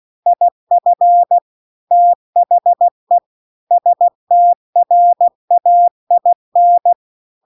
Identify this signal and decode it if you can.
Morse code, heard as a keyed tone, about 16 wpm, 700 Hz. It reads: IF THE STRAIN